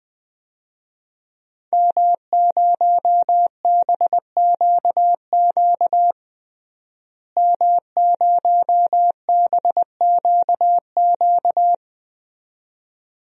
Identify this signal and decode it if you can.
Morse code, heard as a keyed tone, about 20 wpm, 700 Hz. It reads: M0BQQ M0BQQ
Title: DXCC-100 Call Signs - 5 Characters - Repeated 2x - 20wpm